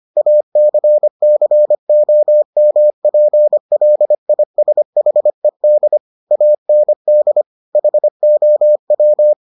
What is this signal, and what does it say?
Morse code, heard as a keyed tone, about 25 wpm, 600 Hz. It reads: ACCOMPLISHED AND HOW